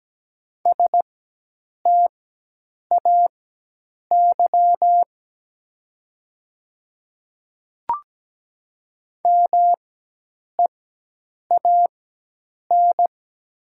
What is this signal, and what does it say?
Morse code, heard as a keyed tone, about 17 wpm, spaced as for 8 wpm, 700 Hz. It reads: STAY MEAN
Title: 4x Character Spacing - Top 500 Words - 17wpm